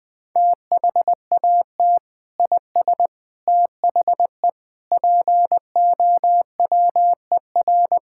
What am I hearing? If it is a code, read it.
Morse code, heard as a keyed tone, about 21 wpm, 700 Hz. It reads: THAT IS THE POWER
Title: Sentences from Top 1000 Words - Review Part 5 - 20wpm